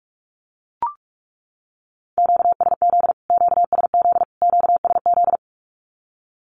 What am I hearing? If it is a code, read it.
Morse code, heard as a keyed tone, about 44 wpm, 700 Hz. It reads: QSZ QSZ QSZ
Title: Q-Codes Repeated 3x 45wpm